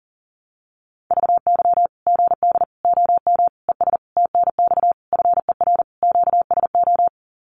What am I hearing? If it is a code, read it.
Morse code, heard as a keyed tone, about 40 wpm, 700 Hz. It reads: VY GD OM ES TNX FER QSO
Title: Common QSO Phrases 40wpm